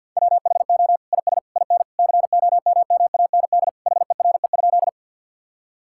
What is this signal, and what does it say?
Morse code, heard as a keyed tone, about 50 wpm, 700 Hz. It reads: WHO IS IN COMMAND HERE?
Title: Sentences from Top 1600 Words 50wpm